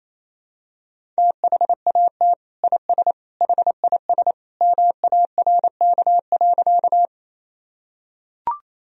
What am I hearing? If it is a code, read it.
Morse code, heard as a keyed tone, about 28 wpm, 700 Hz. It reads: THAT IS HIS MARK.
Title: Sentences from Top 1000 Words - Review Part 1 - 28wpm